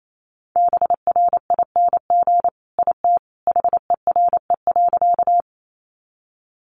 Morse code, heard as a keyed tone, about 28 wpm, 700 Hz. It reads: BRING IT HERE.
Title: Sentences from Top 500 Words 28wpm